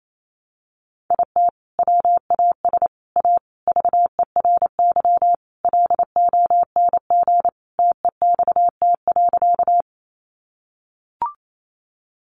Morse code, heard as a keyed tone, about 28 wpm, 700 Hz. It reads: IT WAS A VERY LONG TEXT.
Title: Sentences from Top 1200 Words 28wpm